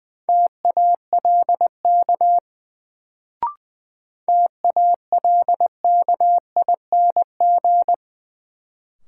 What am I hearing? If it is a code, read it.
Morse code, heard as a keyed tone, about 20 wpm, 700 Hz. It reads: TALK TALKING
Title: Sets of 2 Words - New Word Drill - 20wpm